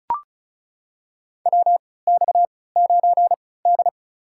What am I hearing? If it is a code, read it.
Morse code, heard as a keyed tone, about 36 wpm, spaced as for 20 wpm, 700 Hz. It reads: WX9D
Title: USA Call Signs - 4 Characters - 3x Character Spacing - 35wpm